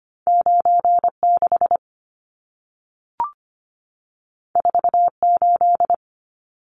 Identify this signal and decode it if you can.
Morse code, heard as a keyed tone, about 25 wpm, 700 Hz. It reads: 96 48
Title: Numbers - 2-Digit 25wpm